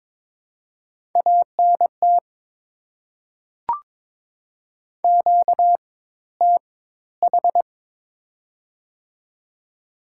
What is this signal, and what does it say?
Morse code, heard as a keyed tone, about 21 wpm, spaced as for 14 wpm, 690 Hz. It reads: ANT QTH
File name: Top 20 QSO Elements - 4x Character Spacing - 22wpm